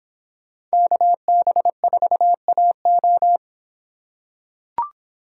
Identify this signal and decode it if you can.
Morse code, heard as a keyed tone, about 26 wpm, 700 Hz. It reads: KB4AO